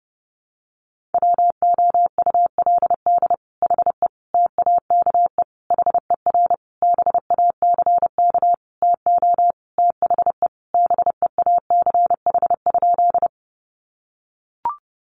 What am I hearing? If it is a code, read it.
Morse code, heard as a keyed tone, about 30 wpm, 700 Hz. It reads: WOULD HE TAKE HER BACK TO THE BEACH?
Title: Sentences from Top 1500 Words 30wpm